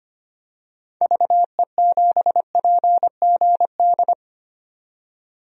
Morse code, heard as a keyed tone, about 25 wpm, 700 Hz. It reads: VE7PGD
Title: DXCC-100 Call Signs - 6 Characters - 25wpm